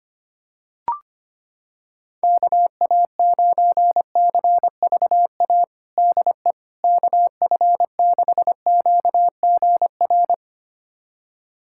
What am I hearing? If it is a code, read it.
Morse code, heard as a keyed tone, about 25 wpm, 700 Hz. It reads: KA9CVA DE KF6QGR
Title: USA Callsign DE Callsign 25wpm